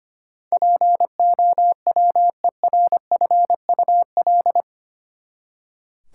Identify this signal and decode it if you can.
Morse code, heard as a keyed tone, about 25 wpm, 700 Hz. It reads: POWERFUL
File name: Eight Letter Words - 25wpm